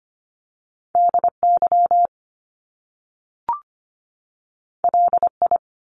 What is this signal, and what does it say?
Morse code, heard as a keyed tone, about 25 wpm, 700 Hz. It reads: DY LI
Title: Top Two Letter Combinations in Words 25wpm